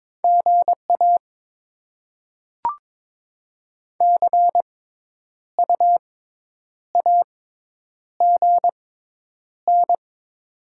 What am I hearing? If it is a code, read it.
Morse code, heard as a keyed tone, about 21 wpm, spaced as for 9 wpm, 700 Hz. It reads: GA CU AGN
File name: Top 20 QSO Elements - 6x Character Spacing - 22wpm